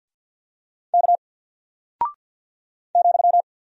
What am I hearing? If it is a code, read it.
Morse code, heard as a keyed tone, about 48 wpm, 700 Hz. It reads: K ,
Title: Single Character (Letter:Number:Punctuation:Prosign) Alt 50wpm